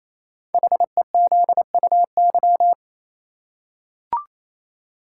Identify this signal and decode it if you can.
Morse code, heard as a keyed tone, about 29 wpm, 700 Hz. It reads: HEZUY
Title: Code Groups - 5 Letters - 28wpm